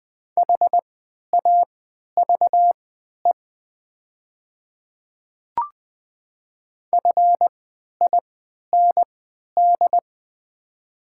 Morse code, heard as a keyed tone, about 20 wpm, spaced as for 11 wpm, 700 Hz. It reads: HAVE FIND
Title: 3x Character Spacing - Top 100 Words - 20wpm